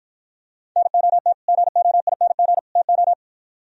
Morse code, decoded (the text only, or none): NOT GOING TO